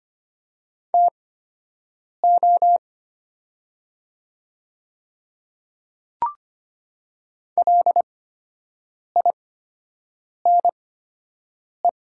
Morse code, heard as a keyed tone, about 25 wpm, spaced as for 7 wpm, 700 Hz. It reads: TO LINE
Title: 8x Character Spacing - Top 300 Words - 25wpm